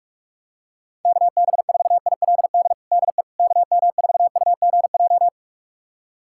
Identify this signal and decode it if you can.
Morse code, heard as a keyed tone, about 46 wpm, 700 Hz. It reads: KB4ILD DE KM4UGJ
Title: USA Callsign DE Callsign - 6 Characters - 45wpm